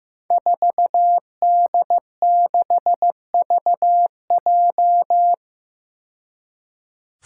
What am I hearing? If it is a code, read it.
Morse code, heard as a keyed tone, about 15 wpm, 700 Hz. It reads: 4D6VJ
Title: Five Challenging Letter-Number Combinations 15wpm